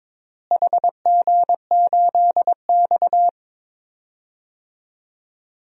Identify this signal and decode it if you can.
Morse code, heard as a keyed tone, about 22 wpm, 700 Hz. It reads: HG8X
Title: DXCC-100 Call Signs - 4 Characters - Repeated 2x - 22wpm